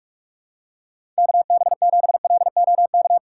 Morse code, heard as a keyed tone, about 46 wpm, 700 Hz. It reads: KB7LOK
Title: USA Call Signs - 4x Character Spacing - 45wpm